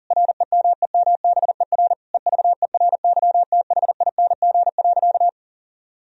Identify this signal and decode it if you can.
Morse code, heard as a keyed tone, about 37 wpm, 700 Hz. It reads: REMEMBER EVERYTHING.
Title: Sentences from Top 1000 Words - Review Part 1 - 40wpm